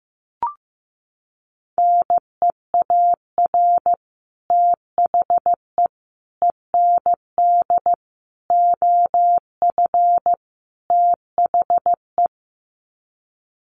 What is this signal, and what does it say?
Morse code, heard as a keyed tone, about 15 wpm, 700 Hz. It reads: NEAR THE END OF THE
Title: Sets of 5 Words Encore 15wpm